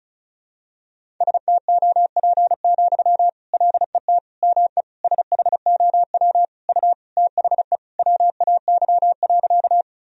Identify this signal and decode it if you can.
Morse code, heard as a keyed tone, about 34 wpm, 700 Hz. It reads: STOP, LET ME SHOW U THE WAY.
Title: Sentences from Top 300 Words and Abbreviations 35wpm